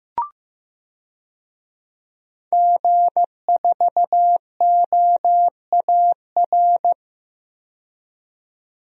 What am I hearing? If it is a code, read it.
Morse code, heard as a keyed tone, about 15 wpm, 700 Hz. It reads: G4OAR
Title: DXCC-100 Call Signs - 5 Characters - 15wpm